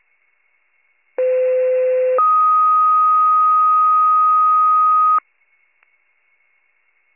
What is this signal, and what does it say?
Two-tone page, heard as a one-second tone, then a three-second tone, about 510 then 1250 Hz. It is heard from 1.2 to 5.2 s.